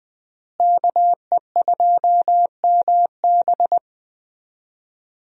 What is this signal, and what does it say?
Morse code, heard as a keyed tone, about 20 wpm, 700 Hz. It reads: KE2MB
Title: USA Call Signs - 5 Characters - Repeated 2x - 20wpm